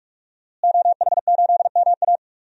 Morse code, heard as a keyed tone, about 41 wpm, 700 Hz. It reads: OH8MA
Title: DXCC-100 Call Signs - 5 Characters - Repeated 3x - 45wpm